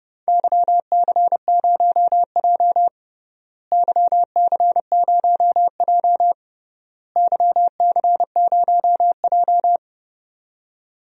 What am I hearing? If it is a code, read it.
Morse code, heard as a keyed tone, about 30 wpm, 700 Hz. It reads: YC0J YC0J YC0J
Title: DXCC-100 Call Signs - 4 Characters - Repeated 3x - 30wpm